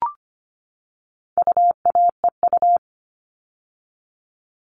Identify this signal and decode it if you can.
Morse code, heard as a keyed tone, about 25 wpm, 700 Hz. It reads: UAEU
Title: All Cut Numbers - 4-Digit 25wpm